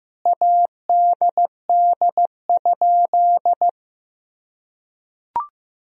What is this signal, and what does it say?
Morse code, heard as a keyed tone, about 15 wpm, 700 Hz. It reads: ADD?
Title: Sentences from Top 1000 Words - Review Part 1 - 15wpm